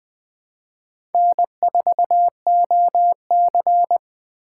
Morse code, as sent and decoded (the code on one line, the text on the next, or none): -. ....- --- -.-.
N4OC